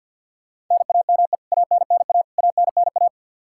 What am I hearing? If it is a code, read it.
Morse code, heard as a keyed tone, about 52 wpm, 700 Hz. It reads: NAME ANNA ANNA
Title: K1USN SST QSO Practice 50wpm